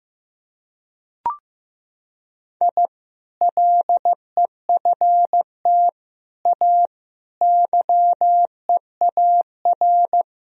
Morse code, heard as a keyed tone, about 15 wpm, 700 Hz. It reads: I LEFT A YEAR
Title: Sentences from Top 1000 Words - Review Part 4 - 15wpm